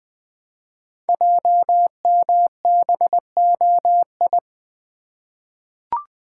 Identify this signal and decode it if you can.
Morse code, heard as a keyed tone, about 20 wpm, 700 Hz. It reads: JMBOI
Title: Code Groups - 5 Letters - 20wpm